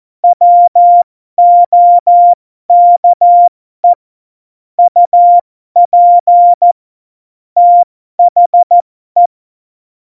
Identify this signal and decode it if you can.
Morse code, heard as a keyed tone, about 14 wpm, spaced as for 12 wpm, 700 Hz. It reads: WOKE UP THE
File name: Morsekurs / Witze